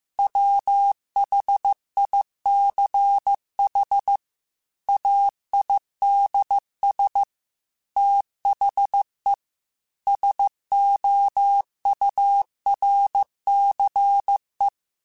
Morse code, heard as a keyed tone, about 15 wpm, 800 Hz. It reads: WHICH AIDS THE SOURCE